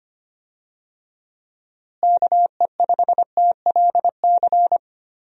Morse code, heard as a keyed tone, about 25 wpm, 700 Hz. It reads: KE5TLC
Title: USA Call Signs - 6 Characters - Repeated 3x - 25wpm